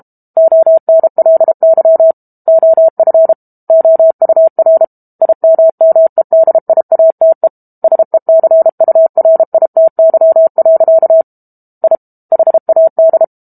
Morse code, heard as a keyed tone, about 32 wpm, 650 Hz. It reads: ONLY OF OUR IMMEDIATE SECURITY. I HAD
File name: Book Two / 06-Chapter 6 - 32 WPM